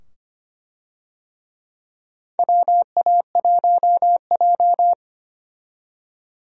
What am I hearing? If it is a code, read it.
Morse code, heard as a keyed tone, about 25 wpm, 700 Hz. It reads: WA1J
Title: USA Call Signs - 4 Characters - 25wpm